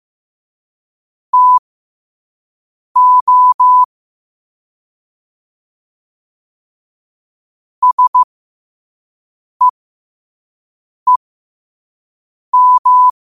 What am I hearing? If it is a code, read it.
Morse code, heard as a keyed tone, about 15 wpm, spaced as for 5 wpm, 1000 Hz. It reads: TO SEEM
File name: qotdfiles / 5wpm